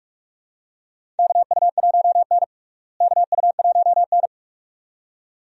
Morse code, heard as a keyed tone, about 46 wpm, 700 Hz. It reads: KU1N KU1N
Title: USA Call Signs - 4 Characters - Repeated 2x - 45wpm